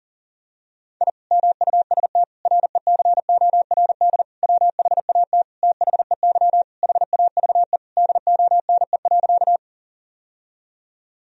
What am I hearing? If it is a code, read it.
Morse code, heard as a keyed tone, about 40 wpm, 700 Hz. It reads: I MUST RECORD WHAT THEY HAVE DONE.
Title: Sentences from Top 1000 Words - Review Part 3 - 40wpm